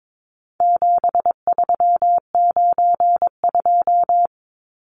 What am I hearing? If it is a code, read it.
Morse code, heard as a keyed tone, about 22 wpm, 700 Hz. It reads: 7392
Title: Numbers - 4-Digit 22wpm